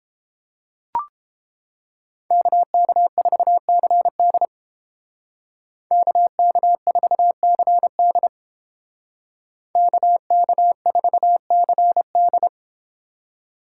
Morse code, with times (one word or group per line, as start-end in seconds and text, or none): 2.3-4.5 KK4CD
5.9-8.3 KK4CD
9.8-12.5 KK4CD